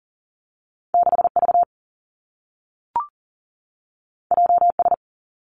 Morse code, heard as a keyed tone, about 40 wpm, 700 Hz. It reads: 6V JS
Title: Two Challenging Letter-Number Combinations 40wpm